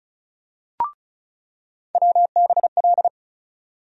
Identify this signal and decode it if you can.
Morse code, heard as a keyed tone, about 35 wpm, 700 Hz. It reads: WBL
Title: USA Call Sign Postfixes - 3 Letters 35wpm